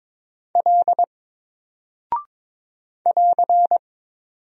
Morse code, heard as a keyed tone, about 22 wpm, 700 Hz. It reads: L <AR>
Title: Single Character (Letter:Number:Punctuation:Prosign) 22wpm